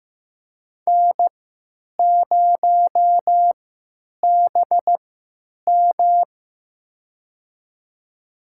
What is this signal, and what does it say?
Morse code, heard as a keyed tone, about 15 wpm, spaced as for 8 wpm, 700 Hz. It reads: N0BM
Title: USA Call Signs - 4 Characters - 3x Character Spacing - 15wpm